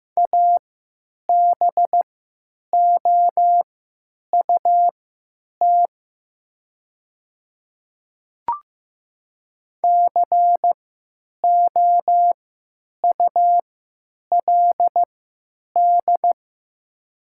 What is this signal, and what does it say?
Morse code, heard as a keyed tone, about 15 wpm, spaced as for 9 wpm, 690 Hz. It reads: ABOUT COULD
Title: 3x Character Spacing - Top 100 Words - 15wpm